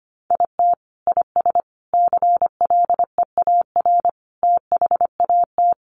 Morse code, heard as a keyed tone, about 25 wpm, 700 Hz. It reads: IT IS CLEAR THAT
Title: Sets of 5 Words Encore 25wpm